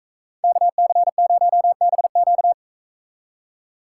Morse code, heard as a keyed tone, about 41 wpm, 700 Hz. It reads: KC0BQ